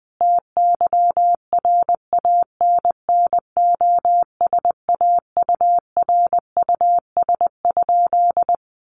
Morse code, heard as a keyed tone, about 20 wpm, 700 Hz. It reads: TYRANNOSAURUS?
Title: A Sound Of Thunder 20wpm